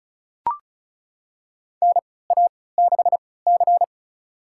Morse code, heard as a keyed tone, about 36 wpm, spaced as for 20 wpm, 700 Hz. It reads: NA6C
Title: USA Call Signs - 4 Characters - 3x Character Spacing - 35wpm